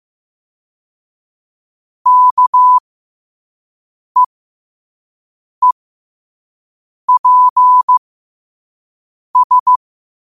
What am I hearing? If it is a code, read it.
Morse code, heard as a keyed tone, about 15 wpm, spaced as for 5 wpm, 1000 Hz. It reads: KEEPS
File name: qotdfiles / 5wpm